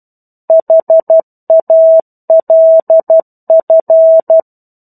Morse code, heard as a keyed tone, about 12 wpm, 650 Hz. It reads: HALF
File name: 01-Chapter-01 12 WPM